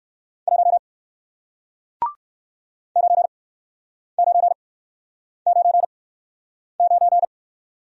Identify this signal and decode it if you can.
Morse code, heard as a keyed tone, about 55 wpm, spaced as for 9 wpm, 700 Hz. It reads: 1 9999
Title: Speed-Racing Single Letter-Number 45wpm